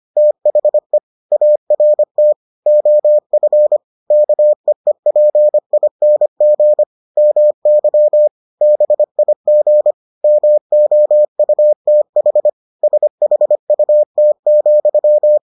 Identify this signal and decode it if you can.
Morse code, heard as a keyed tone, about 25 wpm, 600 Hz. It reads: THE ART OF KEEPING MY BIG MOUTH SHUT,